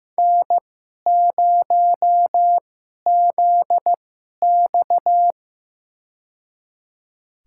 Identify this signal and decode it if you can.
Morse code, heard as a keyed tone, about 15 wpm, spaced as for 11 wpm, 700 Hz. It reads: N0ZX
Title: USA Call Signs - 4 Characters - 2x Character Spacing - 15wpm